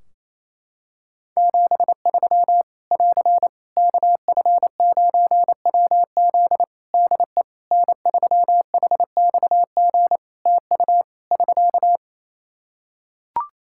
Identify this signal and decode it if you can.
Morse code, heard as a keyed tone, about 28 wpm, 700 Hz. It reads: 73 <AR> KF9WZ DE N3HXG TU <SK>
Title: Practice QSO - Exchange 4 - Ending - 28wpm